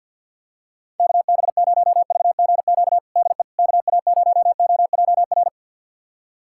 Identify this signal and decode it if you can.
Morse code, heard as a keyed tone, about 53 wpm, 700 Hz. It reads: KB0VGQ DE KA0OJR
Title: USA Callsign DE Callsign 50wpm